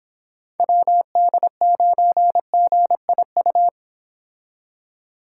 Morse code, heard as a keyed tone, about 29 wpm, 700 Hz. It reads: WD9GIU